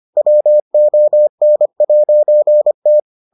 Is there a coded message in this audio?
Yes: Morse code from 0.2 to 3.0 s, heard as a keyed tone, about 25 wpm, 600 Hz. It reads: WON'T